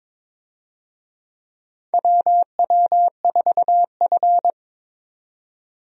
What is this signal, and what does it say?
Morse code, heard as a keyed tone, about 22 wpm, 700 Hz. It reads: WW4F
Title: USA Call Signs - 4 Characters - 22wpm